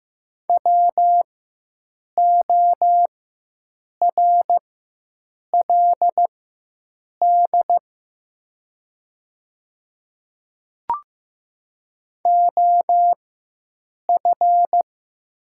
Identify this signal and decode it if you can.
Morse code, heard as a keyed tone, about 15 wpm, spaced as for 7 wpm, 700 Hz. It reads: WORLD OF
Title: 4x Character Spacing - Top 200 Words - 15wpm